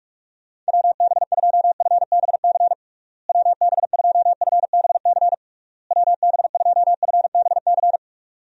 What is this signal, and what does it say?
Morse code, heard as a keyed tone, about 46 wpm, 700 Hz. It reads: WB2FBC WB2FBC WB2FBC
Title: USA Call Signs - 6 Characters - Repeated 3x - 45wpm